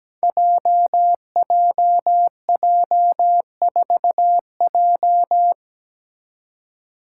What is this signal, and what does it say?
Morse code, heard as a keyed tone, about 17 wpm, 700 Hz. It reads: JJJ4J
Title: Five Challenging Letter-Number Combinations 17wpm